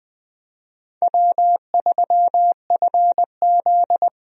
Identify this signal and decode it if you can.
Morse code, heard as a keyed tone, about 20 wpm, 700 Hz. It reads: W3FZ
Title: USA Call Signs - 4 Characters - 20wpm